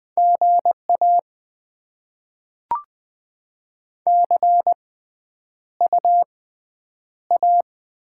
Morse code, heard as a keyed tone, about 20 wpm, spaced as for 9 wpm, 700 Hz. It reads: GA CUA
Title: Top 20 QSO Elements - 6x Character Spacing - 20wpm